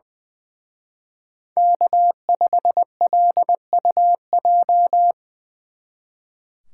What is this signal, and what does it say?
Morse code, heard as a keyed tone, about 23 wpm, 700 Hz. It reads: K5LUJ
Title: Speed Racing - USA Call Signs 20wpm